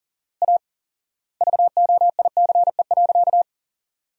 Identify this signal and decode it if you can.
Morse code, heard as a keyed tone, about 37 wpm, 700 Hz. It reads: A VOICE.
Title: 4x Word-Spacing - Sentences from Top 500 Words - 40wpm